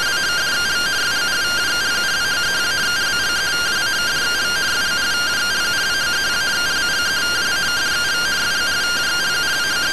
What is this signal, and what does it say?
Two-tone teleprinter signal, about 170 Hz shift.
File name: findings / MR CORONAL 55.gen-22 rtty rx